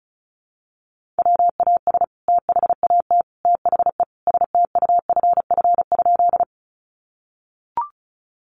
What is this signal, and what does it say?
Morse code, heard as a keyed tone, about 34 wpm, 700 Hz. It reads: WAS THAT THE STUFF?
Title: Sentences from Top 1400 Words 35wpm